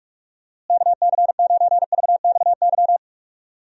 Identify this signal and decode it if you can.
Morse code, heard as a keyed tone, about 46 wpm, 700 Hz. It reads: KC9VXY
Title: USA Call Signs - 6 Characters - Repeated 2x - 45wpm